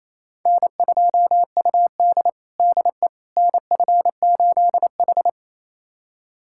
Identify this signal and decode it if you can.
Morse code, heard as a keyed tone, about 27 wpm, 700 Hz. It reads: N2UD DE NF8H
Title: USA Callsign DE Callsign - 4 Characters - 28wpm